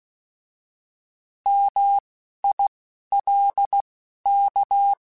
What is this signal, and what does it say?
Morse code, heard as a keyed tone, about 16 wpm, spaced as for 11 wpm, 800 Hz. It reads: MILK
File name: Alphabetical / Alphabetical mp3s / Ms @10